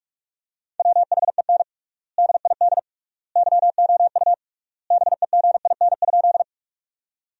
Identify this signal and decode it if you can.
Morse code, heard as a keyed tone, about 46 wpm, 700 Hz. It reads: WHEN DID YOU BEGIN?
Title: 3x Word-Spacing - Sentences from Top 300 Words - 45wpm